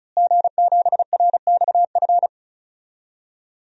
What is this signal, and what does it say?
Morse code, heard as a keyed tone, about 36 wpm, 700 Hz. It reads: G7RXF
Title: DXCC-100 Call Signs - 5 Characters - Repeated 2x - 35wpm